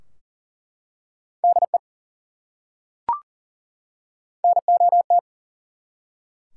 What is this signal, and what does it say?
Morse code, heard as a keyed tone, about 41 wpm, 700 Hz. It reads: DE NOT
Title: Top 180 QSO Elements 40wpm